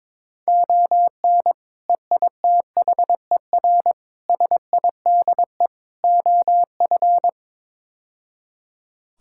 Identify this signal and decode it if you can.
Morse code, heard as a keyed tone, about 22 wpm, 700 Hz. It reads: ON EITHER SIDE OF
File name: practice / 22wpm / Build A Phrase - 5-Word Phrases - 22wpm